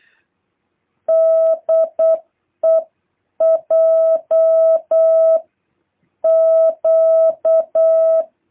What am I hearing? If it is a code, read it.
Morse code, heard as a keyed tone, about 8 wpm, 650 Hz. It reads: DEJ Q
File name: jh8cbh / 2023cw